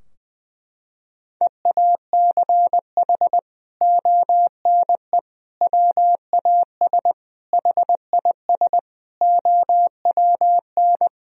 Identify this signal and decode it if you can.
Morse code, heard as a keyed tone, about 20 wpm, 700 Hz. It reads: EACH ONE WAS HIS OWN
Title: Sentences from Top 1000 Words - Review Part 4 - 20wpm